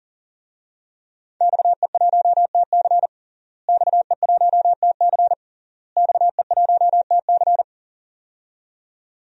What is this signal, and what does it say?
Morse code, heard as a keyed tone, about 38 wpm, 700 Hz. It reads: XE1TC XE1TC XE1TC
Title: DXCC-100 Call Signs - 5 Characters - Repeated 3x - 40wpm